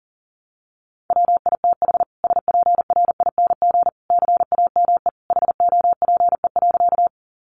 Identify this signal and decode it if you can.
Morse code, heard as a keyed tone, about 40 wpm, 700 Hz. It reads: WITH SPRING CAME HOPE.
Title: Sentences from Top 1200 Words 40wpm